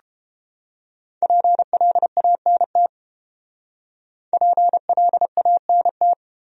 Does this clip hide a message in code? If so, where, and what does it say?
Yes: Morse code from 1.2 to 6.1 s, heard as a keyed tone, about 30 wpm, 700 Hz. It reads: PLANT PLANT